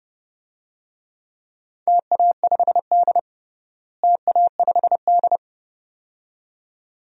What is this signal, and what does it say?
Morse code, heard as a keyed tone, about 32 wpm, 700 Hz. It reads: TA5D TA5D
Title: DXCC-100 Call Signs - 4 Characters - Repeated 2x - 30wpm